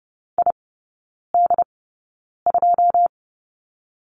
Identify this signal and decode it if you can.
Morse code, heard as a keyed tone, about 30 wpm, spaced as for 9 wpm, 700 Hz. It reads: ID2
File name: 081 Lesson 14e - Number 2 - Three Characters 30wpm